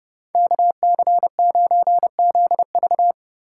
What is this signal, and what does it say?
Morse code, heard as a keyed tone, about 30 wpm, 700 Hz. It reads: KC9ZV